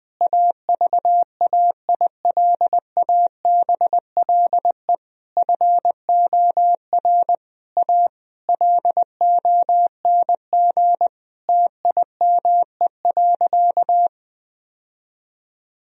Morse code, heard as a keyed tone, about 20 wpm, 700 Hz. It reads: AVAILABLE FOR A LONG TIME.
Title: Sentences from Top 1000 Words - Review Part 4 - 20wpm